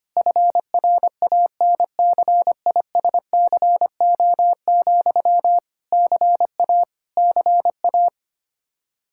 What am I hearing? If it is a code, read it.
Morse code, heard as a keyed tone, about 25 wpm, 700 Hz. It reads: FRANCISCO, CA CA
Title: Practice QSO - Exchange 1 - Beginning - 25wpm